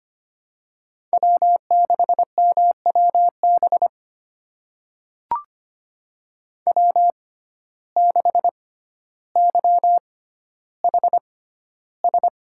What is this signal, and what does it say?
Morse code, heard as a keyed tone, about 25 wpm, spaced as for 13 wpm, 700 Hz. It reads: W6MWB W6YHS
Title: USA Call Signs - 5 Characters - 6x Character Spacing - 25wpm